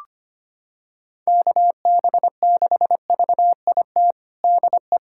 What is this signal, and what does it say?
Morse code, heard as a keyed tone, about 25 wpm, 700 Hz. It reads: KB6VIT DE
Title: USA Callsign DE Callsign - 6 Characters - 25wpm